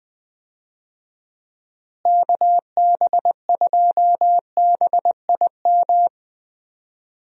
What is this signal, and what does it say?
Morse code, heard as a keyed tone, about 20 wpm, 700 Hz. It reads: KB2BIM
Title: USA Call Signs - 6 Characters - Repeated 2x - 20wpm